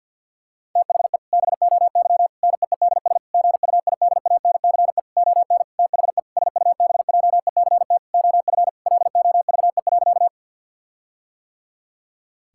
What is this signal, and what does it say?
Morse code, heard as a keyed tone, about 50 wpm, 700 Hz. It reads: THE BOY NEEDS GUIDANCE ON THE SUBJECT OF LOVE.
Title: Sentences from Top 1600 Words 50wpm